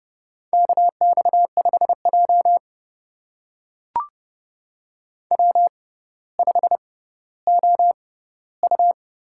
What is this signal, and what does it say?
Morse code, heard as a keyed tone, about 30 wpm, spaced as for 15 wpm, 700 Hz. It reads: KX5J W5OU
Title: USA Call Signs - 4 Characters - 6x Character Spacing - 30wpm